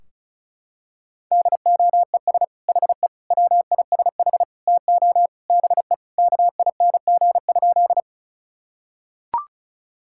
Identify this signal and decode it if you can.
Morse code, heard as a keyed tone, about 36 wpm, 700 Hz. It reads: DOES HE WISH TO BE KING?
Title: Sentences from Top 1600 Words 35wpm